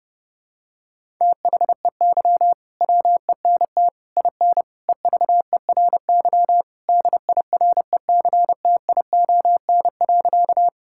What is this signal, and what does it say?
Morse code, heard as a keyed tone, about 30 wpm, 700 Hz. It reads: THEY WENT IN EVERY DIRECTION.
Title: Sentences from Top 1000 Words - Review Part 5 - 30wpm